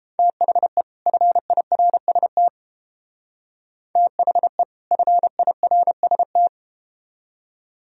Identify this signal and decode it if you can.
Morse code, heard as a keyed tone, about 31 wpm, 700 Hz. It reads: THE FIRST THE FIRST